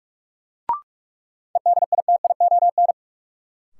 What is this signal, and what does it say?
Morse code, heard as a keyed tone, about 42 wpm, 700 Hz. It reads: EDITION
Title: Sets of 5 Words - New Word Drill - 45wpm